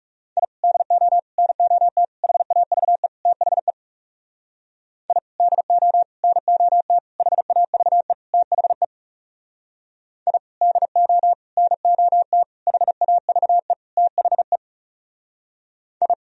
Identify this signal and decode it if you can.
Morse code, heard as a keyed tone, about 38 wpm, 700 Hz. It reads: I DO NOT HAVE THE I DO NOT HAVE THE I DO NOT HAVE THE I